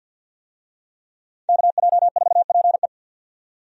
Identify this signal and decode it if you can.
Morse code, heard as a keyed tone, about 49 wpm, 700 Hz. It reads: KJ4PE
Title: USA Call Signs - 5 Characters - 50wpm